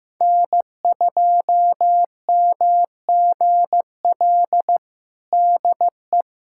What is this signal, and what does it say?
Morse code, heard as a keyed tone, about 16 wpm, 700 Hz. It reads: N2MGL DE
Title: USA Callsign DE Callsign 15wpm